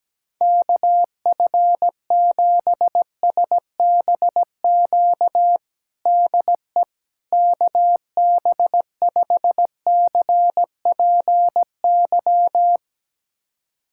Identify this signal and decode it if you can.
Morse code, heard as a keyed tone, about 17 wpm, 700 Hz. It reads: KF7SBQ DE KB5CPY
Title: USA Callsign DE Callsign 17wpm